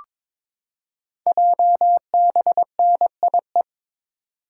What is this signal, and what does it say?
Morse code, heard as a keyed tone, about 23 wpm, 700 Hz. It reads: JBNIE
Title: Code Groups - 5 Letters - 22wpm